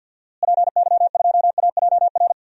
Morse code, heard as a keyed tone, about 51 wpm, 700 Hz. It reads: PY2AJR
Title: DXCC-100 Call Signs - 6 Characters - Repeated 3x - 50wpm